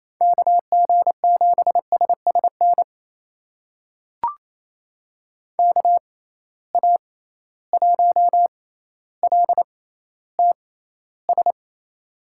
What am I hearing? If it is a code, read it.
Morse code, heard as a keyed tone, about 28 wpm, spaced as for 14 wpm, 700 Hz. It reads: KG7SSN KA1LTS
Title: USA Call Signs - 6x Character Spacing - 28wpm